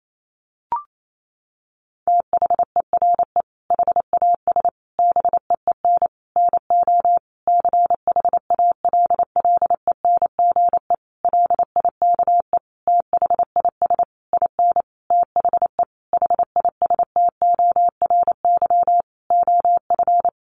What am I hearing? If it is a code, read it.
Morse code, heard as a keyed tone, about 28 wpm, 700 Hz. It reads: THERE HAS BEEN NO CHALLENGE LIKE THIS IN THE HISTORY OF
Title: Sentences from Top 1500 Words 28wpm